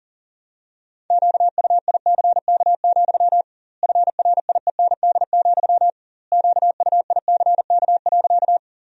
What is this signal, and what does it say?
Morse code, heard as a keyed tone, about 40 wpm, 700 Hz. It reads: QUICK, FRIEND, QUICK.
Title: 2x Word-Spacing - Sentences from Top 500 Words - 40wpm